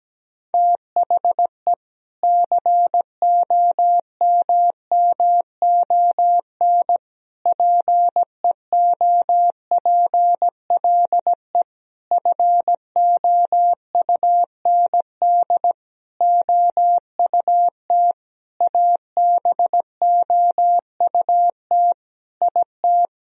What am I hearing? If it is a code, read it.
Morse code, heard as a keyed tone, about 17 wpm, 700 Hz. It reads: THE COMMON PEOPLE FOUND OUT ABOUT IT
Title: Sentences from Top 1000 Words - Review Part 3 - 17wpm